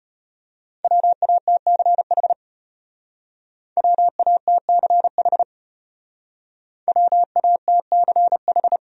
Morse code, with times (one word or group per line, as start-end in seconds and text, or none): 0.8-2.3 WATCH
3.8-5.4 WATCH
6.9-8.8 WATCH